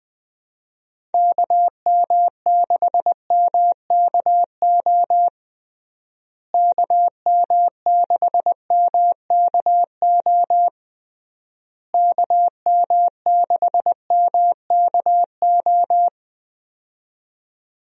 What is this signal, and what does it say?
Morse code, heard as a keyed tone, about 20 wpm, 700 Hz. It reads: KM6MKO KM6MKO KM6MKO
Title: USA Call Signs - 6 Characters - Repeated 3x - 20wpm